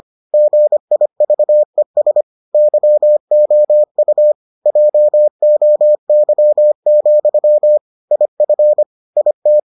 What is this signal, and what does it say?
Morse code, heard as a keyed tone, about 25 wpm, 600 Hz. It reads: GIVES YOU JOY, IF IT